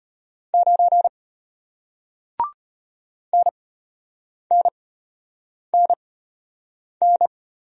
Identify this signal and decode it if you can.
Morse code, heard as a keyed tone, about 33 wpm, spaced as for 8 wpm, 690 Hz. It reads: 9 NNNN